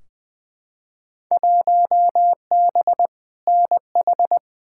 Morse code, heard as a keyed tone, about 20 wpm, 700 Hz. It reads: 1B NH
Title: Field Day Exchanges Encore - 20wpm